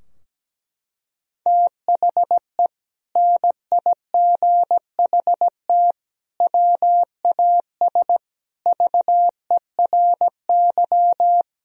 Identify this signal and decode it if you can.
Morse code, heard as a keyed tone, about 17 wpm, 700 Hz. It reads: THE NIGHT WAS VERY
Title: Sentences from Top 1000 Words - Review Part 2 - 17wpm